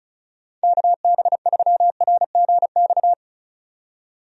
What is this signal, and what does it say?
Morse code, heard as a keyed tone, about 34 wpm, 700 Hz. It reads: KB3RGX
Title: USA Call Signs - 6 Characters - 35wpm